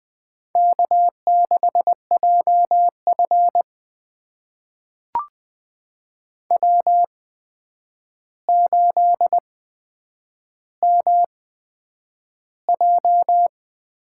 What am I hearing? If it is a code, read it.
Morse code, heard as a keyed tone, about 20 wpm, spaced as for 9 wpm, 700 Hz. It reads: K6JF W8MJ
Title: USA Call Signs - 4 Characters - 8x Character Spacing - 20wpm